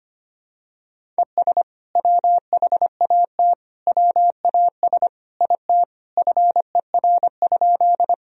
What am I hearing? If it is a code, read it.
Morse code, heard as a keyed tone, about 25 wpm, 700 Hz. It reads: ES WHAT WAS IT FER?
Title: Sentences from Top 100 Words and Abbreviations 25wpm